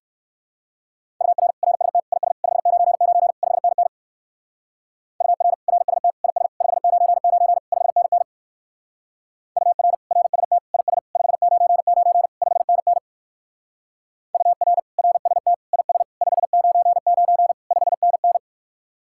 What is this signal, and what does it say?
Morse code, heard as a keyed tone, about 58 wpm, 700 Hz. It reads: UR RST IS 599 5NN UR RST IS 599 5NN UR RST IS 599 5NN UR RST IS 599 5NN